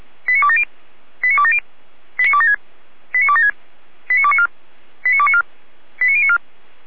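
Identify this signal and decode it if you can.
Five-tone selective call (ZVEI), heard as quick runs of five tones, tones of about 70 ms. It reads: 89270 89270 89276 89276 89294 89294 89094
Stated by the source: ZVEI five-tone selective call received with an SDR, from the Feuerwehr Orlamünde audio server.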